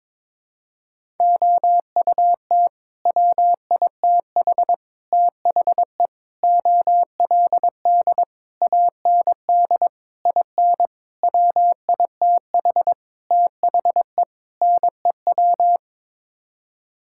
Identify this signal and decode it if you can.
Morse code, heard as a keyed tone, about 22 wpm, 700 Hz. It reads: OUT WITH THE OLD AND IN WITH THE NEW
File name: Sets of 9 Words - 22wpm